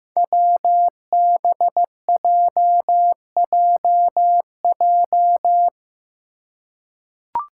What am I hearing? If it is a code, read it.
Morse code, heard as a keyed tone, about 15 wpm, 700 Hz. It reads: WBJJJ
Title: Five Challenging Letter-Number Combinations 15wpm